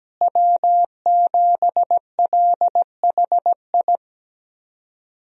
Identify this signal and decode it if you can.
Morse code, heard as a keyed tone, about 17 wpm, 700 Hz. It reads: W7LHI
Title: USA Call Signs - 5 Characters - Repeated 3x - 17wpm